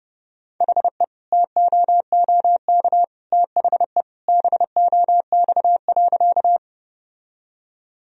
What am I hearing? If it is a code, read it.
Morse code, heard as a keyed tone, about 30 wpm, 700 Hz. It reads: HE TOOK THE BOX.
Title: Sentences from Top 1000 Words - Review Part 2 - 30wpm